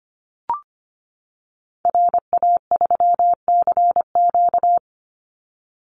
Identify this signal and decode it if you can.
Morse code, heard as a keyed tone, about 25 wpm, 700 Hz. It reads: RA3CQ
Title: Call Signs - 25wpm